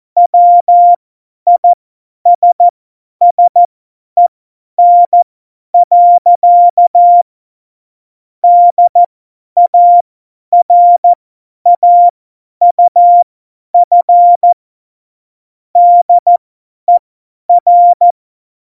Morse code, heard as a keyed tone, about 14 wpm, spaced as for 10 wpm, 700 Hz. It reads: WISSEN. DARAUF DER
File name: Morsekurs / Witze / Witz 05. Februar 2019, 50 F70